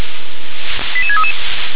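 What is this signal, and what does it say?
Five-tone selective call (ZVEI), heard as quick runs of five tones, tones of about 70 ms. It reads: 88633
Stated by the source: ZVEI five-tone call 88730 received by SDR.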